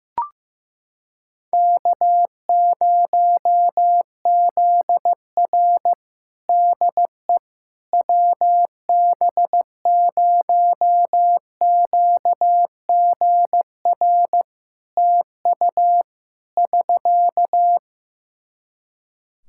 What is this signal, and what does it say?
Morse code, heard as a keyed tone, about 15 wpm, 700 Hz. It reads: K0ZR DE WB0QGR TU <SK>
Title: Practice QSO - Exchange 4 - Ending - 15wpm